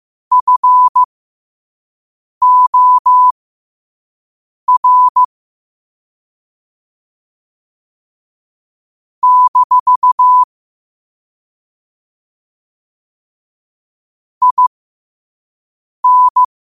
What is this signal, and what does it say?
Morse code, heard as a keyed tone, about 15 wpm, spaced as for 5 wpm, 1000 Hz. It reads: FOR - IN
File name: qotdfiles / 5wpm